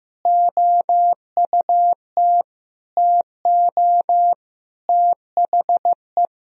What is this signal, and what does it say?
Morse code, heard as a keyed tone, about 15 wpm, 700 Hz. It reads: OUT TO THE
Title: Sentences from Top 1000 Words - Review Part 1 - 15wpm